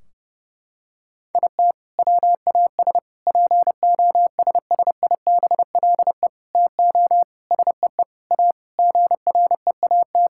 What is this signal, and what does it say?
Morse code, heard as a keyed tone, about 30 wpm, 700 Hz. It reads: IT WAS POSSIBLE TO SEE A GREAT
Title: Sentences from Top 1600 Words 30wpm